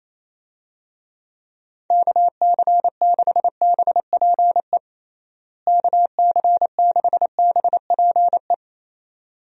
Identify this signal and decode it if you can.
Morse code, heard as a keyed tone, about 28 wpm, 700 Hz. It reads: KC6BPE KC6BPE
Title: USA Call Signs - 6 Characters - Repeated 2x - 28wpm